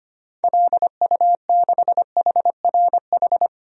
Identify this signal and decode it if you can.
Morse code, heard as a keyed tone, about 25 wpm, 700 Hz. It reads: LU6HRH
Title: DXCC-100 Call Signs - 6 Characters - 25wpm